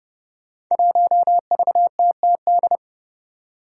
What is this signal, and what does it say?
Morse code, heard as a keyed tone, about 30 wpm, 700 Hz. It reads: 1VTTD
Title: Five Challenging Letter-Number Combinations 30wpm